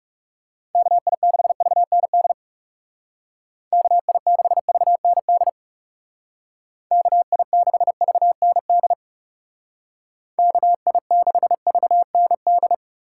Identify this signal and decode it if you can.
Morse code, heard as a keyed tone, about 38 wpm, 700 Hz. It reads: KI6VND KI6VND KI6VND KI6VND
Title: Speed Racing - USA 2x3 Call Signs 30wpm